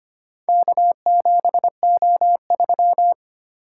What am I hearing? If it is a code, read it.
Morse code, heard as a keyed tone, about 25 wpm, 700 Hz. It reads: K7O3
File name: practice / 25wpm / Four Letter-Number Combinations - 25wpm